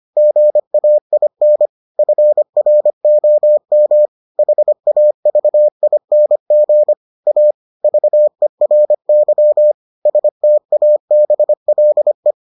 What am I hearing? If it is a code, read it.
Morse code, heard as a keyed tone, about 25 wpm, 600 Hz. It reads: GAIN FROM HAVING A VERY STABLE